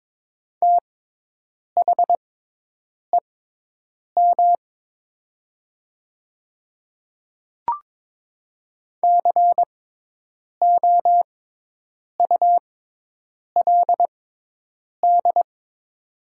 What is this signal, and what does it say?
Morse code, heard as a keyed tone, about 22 wpm, spaced as for 8 wpm, 700 Hz. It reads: THEM COULD
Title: 6x Character Spacing - Top 100 Words - 22wpm